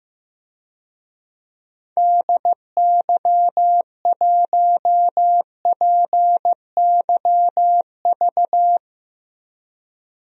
Morse code, heard as a keyed tone, about 15 wpm, 700 Hz. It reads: DY1PYV
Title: DXCC-100 Call Signs - 6 Characters - Repeated 3x - 15wpm